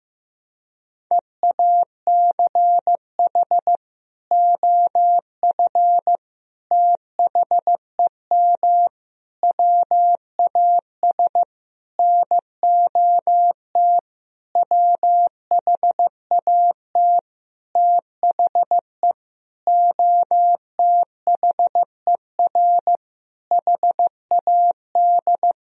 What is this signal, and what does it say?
Morse code, heard as a keyed tone, about 16 wpm, 700 Hz. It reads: EACH OF THEM WAS NOT WHAT THE OTHER HAD
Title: Sentences from Top 1000 Words - Review Part 1 - 15wpm